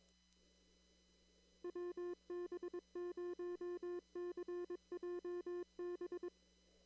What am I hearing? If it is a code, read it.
Morse code, heard as a keyed tone, about 22 wpm, 350 Hz. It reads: WB0CJB
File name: CW / kt5j wpx cw-op-k5pi